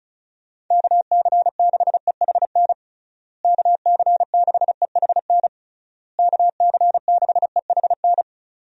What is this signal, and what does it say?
Morse code, heard as a keyed tone, about 34 wpm, 700 Hz. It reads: KC6EHN KC6EHN KC6EHN
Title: USA Call Signs - 6 Characters - Repeated 3x - 35wpm